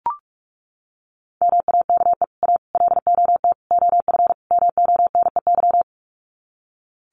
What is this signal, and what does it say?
Morse code, heard as a keyed tone, about 44 wpm, 700 Hz. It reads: MAKE A LOT OF MONEY
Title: Sets of 5 Words Encore2 45wpm